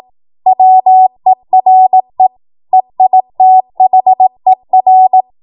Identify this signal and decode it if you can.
Morse code, heard as a keyed tone, about 18 wpm, 750 Hz. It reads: WERE EITHER
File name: Morse / 18WPM